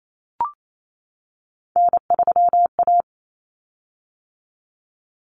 Morse code, heard as a keyed tone, about 28 wpm, 700 Hz. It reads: N3A
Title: Common Cut Numbers - 3-Digit 28wpm